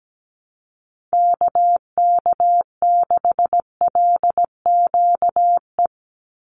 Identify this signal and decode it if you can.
Morse code, heard as a keyed tone, about 17 wpm, 700 Hz. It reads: KK6LQE
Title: USA Call Signs 17wpm